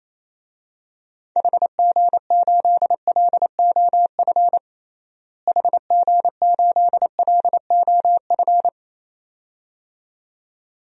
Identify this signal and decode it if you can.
Morse code, heard as a keyed tone, about 28 wpm, 700 Hz. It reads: HG8LOF HG8LOF
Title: DXCC-100 Call Signs - 6 Characters - Repeated 2x - 28wpm